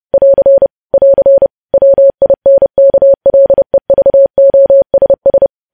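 Morse code, heard as a keyed tone, about 30 wpm, 550 Hz. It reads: <AR> <AR> WINKLEVOSS
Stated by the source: Morse news headlines sent at 30 wpm.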